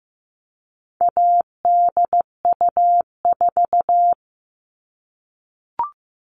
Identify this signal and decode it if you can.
Morse code, heard as a keyed tone, about 15 wpm, 700 Hz. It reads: ADU4
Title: All Cut Numbers - 4-Digit 15wpm